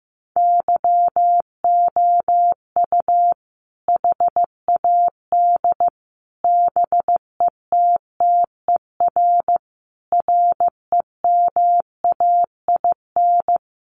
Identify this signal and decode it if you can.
Morse code, heard as a keyed tone, about 14 wpm, 700 Hz. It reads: YOU HAD BETTER REMAIN
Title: Sentences from Top 1200 Words 15wpm